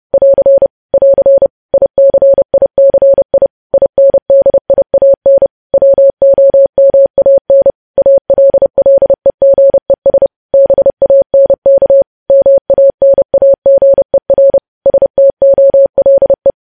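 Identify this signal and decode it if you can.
Morse code, heard as a keyed tone, about 30 wpm, 550 Hz. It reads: <AR> <AR> ICICI INDIAN WOMAN ALLEGES BANK MANAGER STOLE
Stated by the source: News headlines sent in Morse code at 30.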